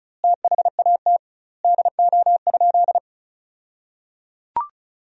Morse code, heard as a keyed tone, about 34 wpm, 700 Hz. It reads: THAT DO?
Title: 2x Word-Spacing - Sentences from Top 100 Words - 35wpm